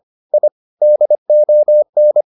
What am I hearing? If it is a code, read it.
Morse code, heard as a keyed tone, about 25 wpm, 600 Hz. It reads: I DON
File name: the old man and the sea part 1 25wpm